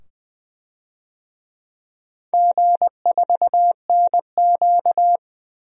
Morse code, heard as a keyed tone, about 20 wpm, 700 Hz. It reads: G4NQ
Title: DXCC-100 Call Signs - 4 Characters - 20wpm